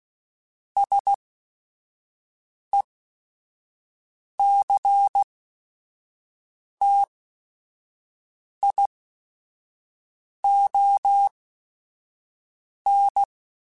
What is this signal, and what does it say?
Morse code, heard as a keyed tone, about 16 wpm, spaced as for 5 wpm, 800 Hz. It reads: SECTION
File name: Technical Subjects / Technical Subjects-mp3s / Sw Pwr Sup @5